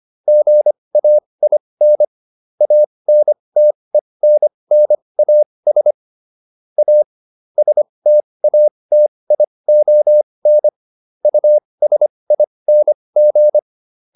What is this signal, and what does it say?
Morse code, heard as a keyed tone, about 25 wpm, spaced as for 20 wpm, 600 Hz. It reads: GAIN ANTENNAS A STATION USING